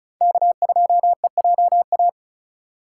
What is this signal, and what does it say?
Morse code, heard as a keyed tone, about 36 wpm, 700 Hz. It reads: K2EJA
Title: USA Call Signs - 5 Characters - 2x Character Spacing - 35wpm